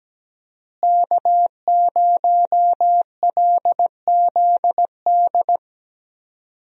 Morse code, heard as a keyed tone, about 17 wpm, 700 Hz. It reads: K0LZD
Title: USA Call Signs - 5 Characters - Repeated 3x - 17wpm